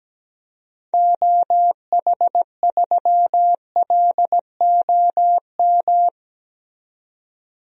Morse code, heard as a keyed tone, about 17 wpm, 700 Hz. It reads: OH3LOM
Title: DXCC-100 Call Signs - 6 Characters - 17wpm